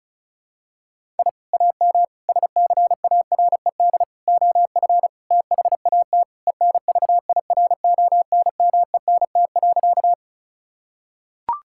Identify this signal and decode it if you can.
Morse code, heard as a keyed tone, about 36 wpm, 700 Hz. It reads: I AM SCARED OF THAT ENVIRONMENT.
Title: Sentences from Top 1000 Words - Review Part 5 - 35wpm